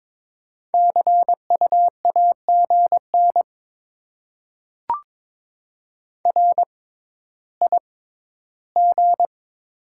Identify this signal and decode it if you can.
Morse code, heard as a keyed tone, about 22 wpm, spaced as for 13 wpm, 690 Hz. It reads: CU AGN RIG
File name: Top 20 QSO Elements - 6x Character Spacing - 22wpm